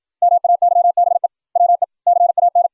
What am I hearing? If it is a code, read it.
Morse code, heard as a keyed tone, about 56 wpm, 700 Hz. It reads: MAYBE WE CAN